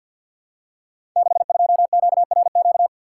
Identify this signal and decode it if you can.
Morse code, heard as a keyed tone, about 51 wpm, 700 Hz. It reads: 62QRQ
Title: Code Groups - 5 Letter:Numbers - 50wpm